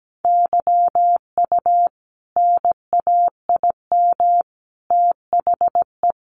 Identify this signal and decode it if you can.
Morse code, heard as a keyed tone, about 17 wpm, 700 Hz. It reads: YU NAIM THE
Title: A Sound Of Thunder 17wpm